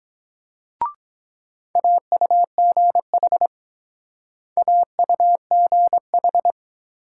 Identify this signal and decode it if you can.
Morse code, heard as a keyed tone, about 22 wpm, 700 Hz. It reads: AUGH AUGH